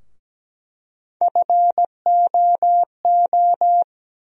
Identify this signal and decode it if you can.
Morse code, heard as a keyed tone, about 17 wpm, 700 Hz. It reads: FOO
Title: USA Call Sign Postfixes - 3 Letters 17wpm